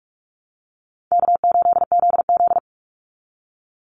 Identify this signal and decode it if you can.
Morse code, heard as a keyed tone, about 44 wpm, 700 Hz. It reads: K8ZZ
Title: Easy Call Signs Repeated 2x - 45wpm